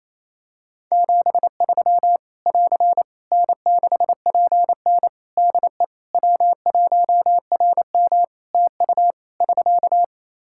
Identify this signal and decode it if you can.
Morse code, heard as a keyed tone, about 28 wpm, 700 Hz. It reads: 73 <AR> N6PN DE W1RM TU <SK>
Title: Practice QSO - Exchange 4 - Ending - 28wpm